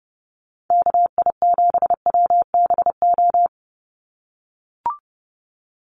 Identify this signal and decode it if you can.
Morse code, heard as a keyed tone, about 29 wpm, 700 Hz. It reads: KI7WBO
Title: USA Call Signs 30wpm